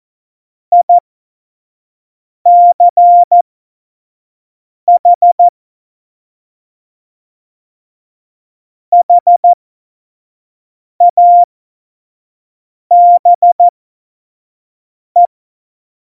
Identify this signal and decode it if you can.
Morse code, heard as a keyed tone, about 14 wpm, spaced as for 5 wpm, 700 Hz. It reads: ICH HABE
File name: Morsekurs / Witze / Witz 04. Februar 2020, 25 F70